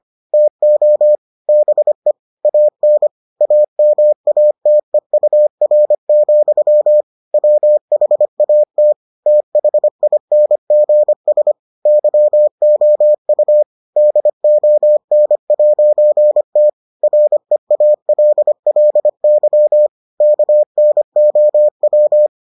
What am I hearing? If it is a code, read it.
Morse code, heard as a keyed tone, about 25 wpm, 600 Hz. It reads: TO BE AN AMATEUR, WHAT THINGS YOU DON'T REALLY KNOW